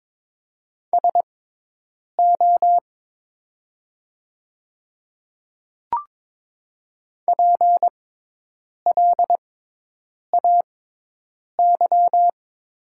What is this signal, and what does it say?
Morse code, heard as a keyed tone, about 22 wpm, spaced as for 8 wpm, 700 Hz. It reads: SO PLAY
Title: 6x Character Spacing - Top 200 Words - 22wpm